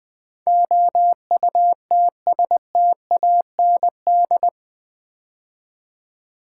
Morse code, heard as a keyed tone, about 20 wpm, 700 Hz. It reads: OUTSTAND
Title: Build A Word - 4 Letters At A Time - 11-Letter Words - 20wpm